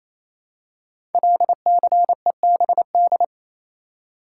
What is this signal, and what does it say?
Morse code, heard as a keyed tone, about 28 wpm, 700 Hz. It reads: LCEBD
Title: Code Groups - 5 Letters - 28wpm